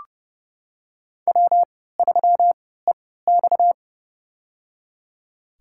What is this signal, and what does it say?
Morse code, heard as a keyed tone, about 30 wpm, spaced as for 17 wpm, 700 Hz. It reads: W3EX
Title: USA Call Signs - 4 Characters - 3x Character Spacing - 30wpm